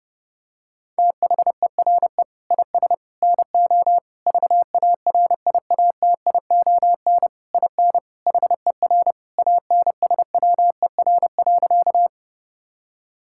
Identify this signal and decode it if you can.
Morse code, heard as a keyed tone, about 30 wpm, 700 Hz. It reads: THERE IS NO VARIATION IN HER ANSWER.
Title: Sentences from Top 1600 Words 30wpm